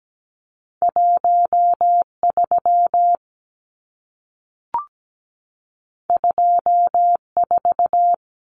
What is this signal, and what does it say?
Morse code, heard as a keyed tone, about 17 wpm, 700 Hz. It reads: 13 24
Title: Numbers - 2-Digit 17wpm